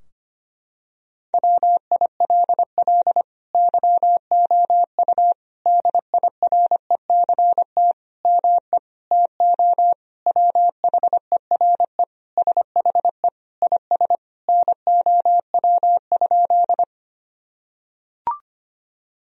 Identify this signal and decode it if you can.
Morse code, heard as a keyed tone, about 25 wpm, 700 Hz. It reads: WILL YOU DIRECT ME TO WHERE SHE IS NOW?
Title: Sentences from Top 1000 Words - Review Part 2 - 25wpm